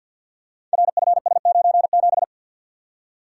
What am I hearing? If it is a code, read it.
Morse code, heard as a keyed tone, about 53 wpm, 700 Hz. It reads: RFS97
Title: Code Groups - 5 Letter:Numbers - 50wpm